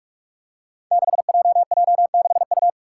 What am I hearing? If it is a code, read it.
Morse code, heard as a keyed tone, about 46 wpm, 700 Hz. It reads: 6JJ6U
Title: Five Challenging Letter-Number Combinations 45wpm